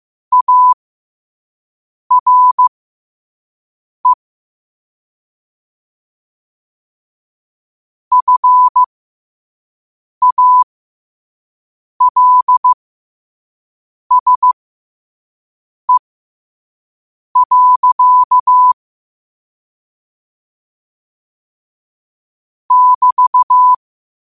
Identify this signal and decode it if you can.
Morse code, heard as a keyed tone, about 15 wpm, spaced as for 5 wpm, 1000 Hz. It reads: ARE FALSE. <BT>
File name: qotdfiles / 5wpm